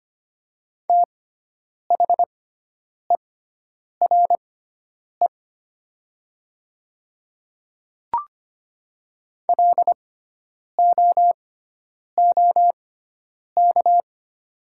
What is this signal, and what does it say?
Morse code, heard as a keyed tone, about 25 wpm, spaced as for 9 wpm, 700 Hz. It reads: THERE LOOK
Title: 6x Character Spacing - Top 200 Words - 25wpm